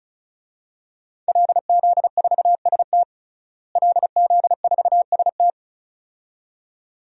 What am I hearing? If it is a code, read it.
Morse code, heard as a keyed tone, about 36 wpm, 700 Hz. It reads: LZ4ST LZ4ST
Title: DXCC-100 Call Signs - 5 Characters - Repeated 2x - 35wpm